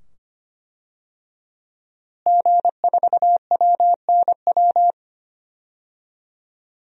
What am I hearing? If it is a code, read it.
Morse code, heard as a keyed tone, about 25 wpm, 700 Hz. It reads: G4WNW
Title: DXCC-100 Call Signs - 5 Characters - Repeated 2x - 25wpm